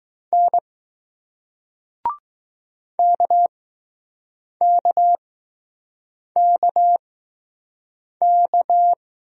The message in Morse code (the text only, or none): N KKKK